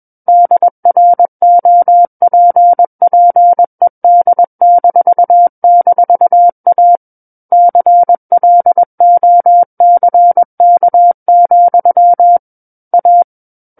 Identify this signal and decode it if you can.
Morse code, heard as a keyed tone, about 21 wpm, 700 Hz. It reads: DROPPED--A CLOCK, A
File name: Book One / 12-Chapter 12 - 21 WPM